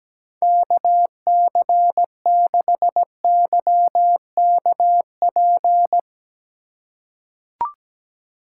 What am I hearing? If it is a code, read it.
Morse code, heard as a keyed tone, about 17 wpm, 700 Hz. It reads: KC6YKP
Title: USA Call Signs - 4x Character Spacing - 17wpm